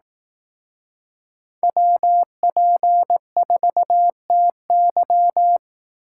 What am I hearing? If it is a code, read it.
Morse code, heard as a keyed tone, about 18 wpm, 700 Hz. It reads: WP4TY
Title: Speed Racing - USA 2x2 Call Signs 15wpm